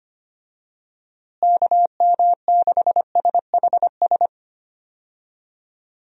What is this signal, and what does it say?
Morse code, heard as a keyed tone, about 25 wpm, 700 Hz. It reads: KM6SHS
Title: USA Call Signs - 6 Characters - Repeated 2x - 25wpm